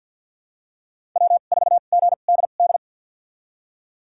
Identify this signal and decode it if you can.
Morse code, heard as a keyed tone, about 53 wpm, spaced as for 37 wpm, 700 Hz. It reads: W4GDD
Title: USA Call Signs - 5 Characters - 2x Character Spacing - 50wpm